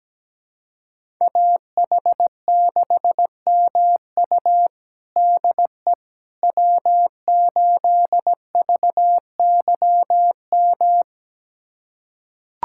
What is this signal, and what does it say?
Morse code, heard as a keyed tone, about 17 wpm, 700 Hz. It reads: AH6MU DE W8VYM
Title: USA Callsign DE Callsign - 5 Characters - 17wpm